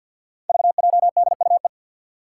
Morse code, heard as a keyed tone, about 53 wpm, 700 Hz. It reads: VJDUE
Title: Five Challenging Letter-Number Combinations 50wpm